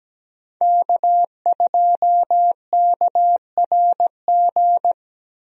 Morse code, heard as a keyed tone, about 17 wpm, 700 Hz. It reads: K2KRG
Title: USA Call Signs - 5 Characters - 17wpm